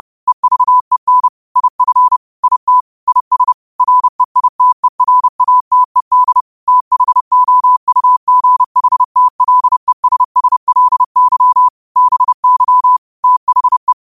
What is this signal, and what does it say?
Morse code, heard as a keyed tone, about 30 wpm, 1000 Hz. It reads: EVEN IF IT IS REITERATED THOUGHTLESSLY BY THE
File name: qotdfiles / 30wpm